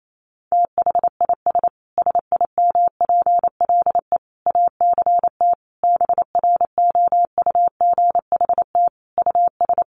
Morse code, heard as a keyed tone, about 28 wpm, 700 Hz. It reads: THIS SIMPLE ACT BROUGHT US
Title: Sentences from Top 1500 Words 28wpm